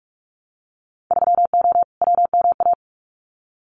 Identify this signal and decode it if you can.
Morse code, heard as a keyed tone, about 46 wpm, 700 Hz. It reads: 3O WMA
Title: Winter Field Day Exchanges Encore 45wpm